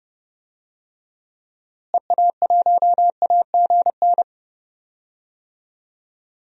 Morse code, heard as a keyed tone, about 30 wpm, 700 Hz. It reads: EA1AGN
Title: DXCC-100 Call Signs - 6 Characters - Repeated 3x - 30wpm